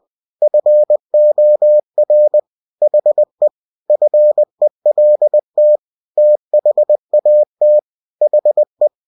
Morse code, heard as a keyed tone, about 20 wpm, 600 Hz. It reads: FOR HE FELT THAT HE
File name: cw-books / winnie-the-pooh / winnie-20wpm-page-0048